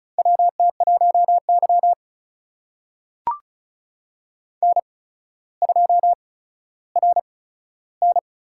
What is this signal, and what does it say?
Morse code, heard as a keyed tone, about 36 wpm, spaced as for 15 wpm, 700 Hz. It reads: WT1Y N2RN
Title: USA Call Signs - 4 Characters - 8x Character Spacing - 35wpm